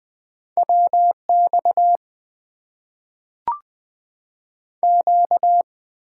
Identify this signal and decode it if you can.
Morse code, heard as a keyed tone, about 20 wpm, 700 Hz. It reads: WX Q
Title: Top 90 QSO Elements - 6x Character Spacing - 20wpm